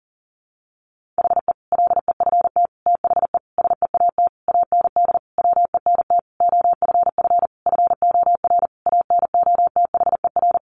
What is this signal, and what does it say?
Morse code, heard as a keyed tone, about 40 wpm, 700 Hz. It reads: HE LEFT THE SEAT AND WENT OFF FOR ANOTHER
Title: Sentences from Top 1500 Words 40wpm